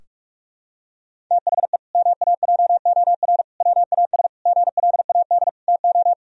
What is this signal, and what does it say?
Morse code, heard as a keyed tone, about 46 wpm, 700 Hz. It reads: THE MAJOR WAS GLAD TO
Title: Sentences from Top 1000 Words - Review Part 4 - 45wpm